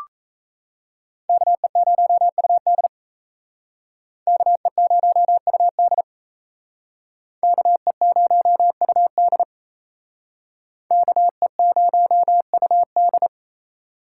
Morse code, heard as a keyed tone, about 37 wpm, 700 Hz. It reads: KE0UD KE0UD KE0UD KE0UD